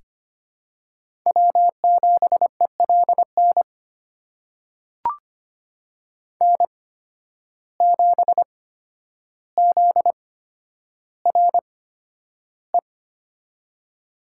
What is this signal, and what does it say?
Morse code, heard as a keyed tone, about 25 wpm, spaced as for 11 wpm, 700 Hz. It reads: W7ELN N7ZRE
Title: USA Call Signs - 5 Characters - 8x Character Spacing - 25wpm